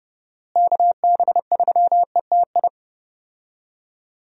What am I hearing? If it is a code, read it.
Morse code, heard as a keyed tone, about 32 wpm, 700 Hz. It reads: KB3ETI
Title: USA Call Signs - 6 Characters - Repeated 3x - 30wpm